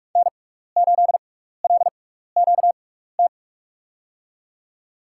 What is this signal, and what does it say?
Morse code, heard as a keyed tone, about 46 wpm, spaced as for 15 wpm, 690 Hz. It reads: N8LQT
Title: USA Call Signs - 5 Characters - 6x Character Spacing - 45wpm